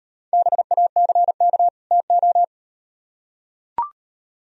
Morse code, heard as a keyed tone, about 36 wpm, 700 Hz. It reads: BACK TO